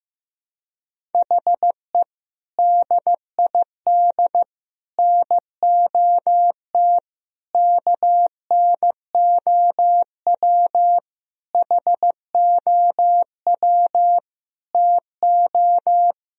Morse code, heard as a keyed tone, about 16 wpm, 700 Hz. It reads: HE DID NOT KNOW HOW TO
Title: Sentences from Top 1000 Words - Review Part 1 - 15wpm